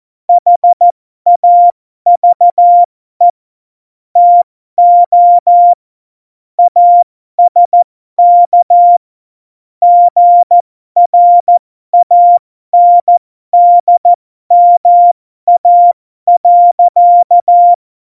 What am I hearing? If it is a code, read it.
Morse code, heard as a keyed tone, about 14 wpm, spaced as for 12 wpm, 700 Hz. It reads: HAVE TO ASK GRANDMA.
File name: Morsekurs / Witze